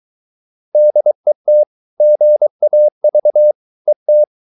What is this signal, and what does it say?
Morse code, heard as a keyed tone, about 23 wpm, 600 Hz. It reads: DET GAV ET
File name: Ferie QRS 23wpm